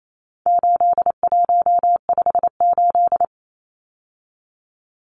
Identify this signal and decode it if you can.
Morse code, heard as a keyed tone, about 28 wpm, 700 Hz. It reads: 8158
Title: Numbers - 4-Digit 28wpm